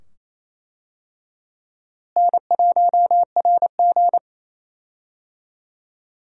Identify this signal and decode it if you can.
Morse code, heard as a keyed tone, about 28 wpm, 700 Hz. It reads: N1RG
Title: USA Call Signs - 4 Characters - Repeated 2x - 28wpm